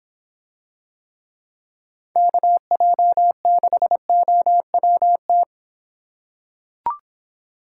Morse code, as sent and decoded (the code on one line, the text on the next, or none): -.- .--- -.... --- .-- -
KJ6OWT